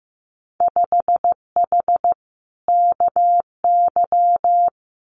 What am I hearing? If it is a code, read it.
Morse code, heard as a keyed tone, about 15 wpm, 700 Hz. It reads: 5H KY
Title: Winter Field Day Exchanges Encore 15wpm